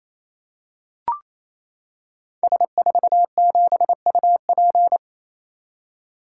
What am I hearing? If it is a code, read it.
Morse code, heard as a keyed tone, about 28 wpm, 700 Hz. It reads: S47UP
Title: Code Groups - 5 Letter:Numbers - 28wpm